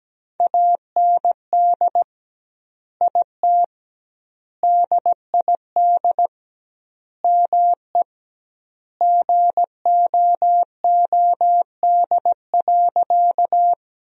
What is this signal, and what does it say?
Morse code, heard as a keyed tone, about 17 wpm, 700 Hz. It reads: AND IT DID ME GOOD.
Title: 2x Word-Spacing - Sentences from Top 200 Words - 17wpm